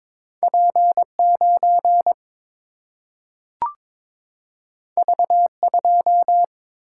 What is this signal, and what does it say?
Morse code, heard as a keyed tone, about 22 wpm, 700 Hz. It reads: P 9 V 2
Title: Single-Letter Single-Number 22wpm